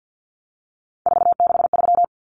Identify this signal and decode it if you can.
Morse code, heard as a keyed tone, about 50 wpm, 700 Hz. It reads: 463
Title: Common Cut Numbers - 3-Digit 50wpm